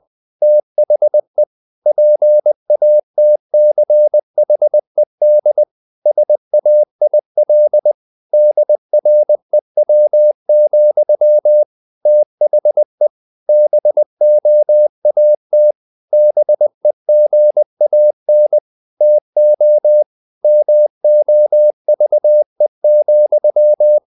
Morse code, heard as a keyed tone, about 20 wpm, 600 Hz. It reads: THE PATCHED SAIL DREW, THE BOAT BEGAN TO MOVE,
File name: the old man and the sea part 2 20wpm